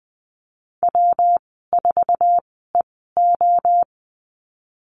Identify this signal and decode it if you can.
Morse code, heard as a keyed tone, about 19 wpm, spaced as for 14 wpm, 700 Hz. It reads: W4EO
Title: 128 Lesson 21g - Number 6 - Call Signs 20wpm